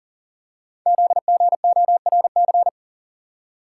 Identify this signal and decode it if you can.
Morse code, heard as a keyed tone, about 40 wpm, 700 Hz. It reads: ZGORC
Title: Code Groups - 5 Letters - 40wpm